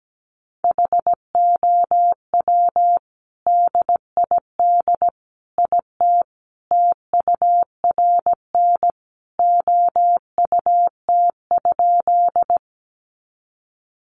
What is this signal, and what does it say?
Morse code, heard as a keyed tone, about 17 wpm, 700 Hz. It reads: HOW DID IT TURN OUT?
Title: A Sound Of Thunder 17wpm